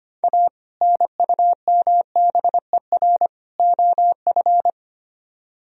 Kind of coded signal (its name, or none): Morse code